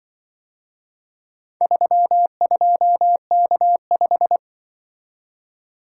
Morse code, heard as a keyed tone, about 25 wpm, 700 Hz. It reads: 32K5
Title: Speed-Racing - Four Letter-Number Combinations - 20wpm